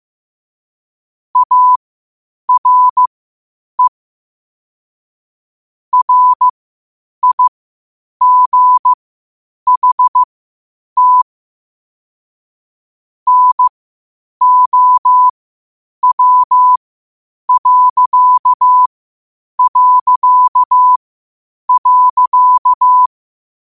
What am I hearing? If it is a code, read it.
Morse code, heard as a keyed tone, about 15 wpm, spaced as for 8 wpm, 1000 Hz. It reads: ARE RIGHT NOW...
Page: Quotes for Thu, 14 Aug 2025 in Morse Code at 8 words per minute.